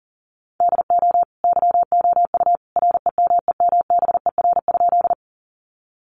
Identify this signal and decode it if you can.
Morse code, heard as a keyed tone, about 40 wpm, 700 Hz. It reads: DO YOU REMEMBER?
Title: Sentences from Top 400 Words 40wpm